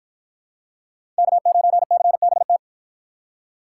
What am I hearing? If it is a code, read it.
Morse code, heard as a keyed tone, about 52 wpm, 700 Hz. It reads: K9CBT